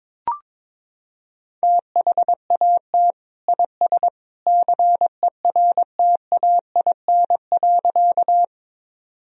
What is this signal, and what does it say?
Morse code, heard as a keyed tone, about 22 wpm, 700 Hz. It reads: THAT IS CERTAIN.
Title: Sentences from Top 1000 Words - Review Part 2 - 22wpm